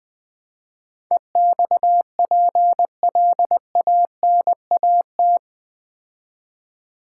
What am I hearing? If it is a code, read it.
Morse code, heard as a keyed tone, about 20 wpm, 700 Hz. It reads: EXPLANAT
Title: Build A Word - 4 Letters At A Time - 11-Letter Words - 20wpm